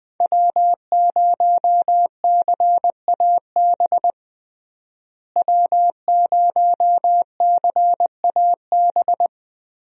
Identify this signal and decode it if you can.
Morse code, heard as a keyed tone, about 20 wpm, 700 Hz. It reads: W0CAB W0CAB
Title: USA Call Signs - 5 Characters - Repeated 3x - 20wpm